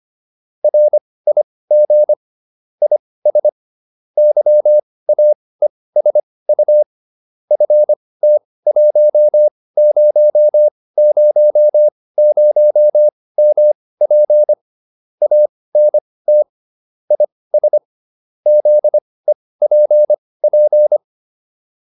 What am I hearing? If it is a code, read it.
Morse code, heard as a keyed tone, about 25 wpm, spaced as for 18 wpm, 600 Hz. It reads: RIG IS YAESU FT1000MP ANT IS ZEPP
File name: QSO401 18